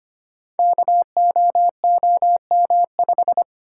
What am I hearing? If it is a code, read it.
Morse code, heard as a keyed tone, about 25 wpm, 700 Hz. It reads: KOOM5
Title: Code Groups - 5 Letter:Numbers - 25wpm